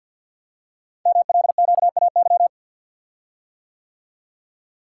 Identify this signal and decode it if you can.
Morse code, heard as a keyed tone, about 48 wpm, 700 Hz. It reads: MLQAY
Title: Code Groups - 5 Letters - 50wpm